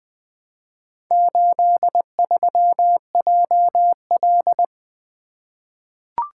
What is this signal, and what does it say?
Morse code, heard as a keyed tone, about 20 wpm, 700 Hz. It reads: 83JL
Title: Four Letter-Number Combinations - 20wpm